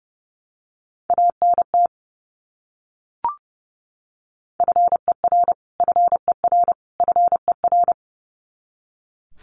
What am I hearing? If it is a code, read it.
Morse code, heard as a keyed tone, about 30 wpm, 700 Hz. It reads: ANT FER FER FER
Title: Abbreviations Repeated 3x - 30wpm